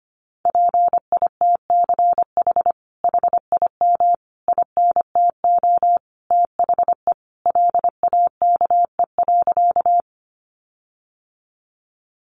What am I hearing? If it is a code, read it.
Morse code, heard as a keyed tone, about 25 wpm, 700 Hz. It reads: PITCH HIM INTO THE LAKE.
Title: Sentences from Top 1300 Words 25wpm